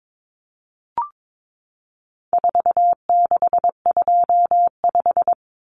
Morse code, heard as a keyed tone, about 22 wpm, 700 Hz. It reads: 4625
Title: Common Cut Numbers - 4-Digit 22wpm